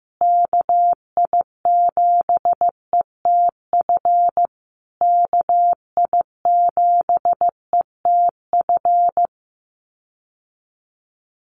Morse code, heard as a keyed tone, about 15 wpm, 700 Hz. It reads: KI7ETF KI7ETF
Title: USA Call Signs Repeated 3x 15wpm